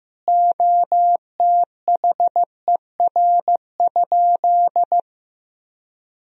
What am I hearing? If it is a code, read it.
Morse code, heard as a keyed tone, about 15 wpm, 700 Hz. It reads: OTHER?
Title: Sentences from Top 1000 Words - Review Part 3 - 15wpm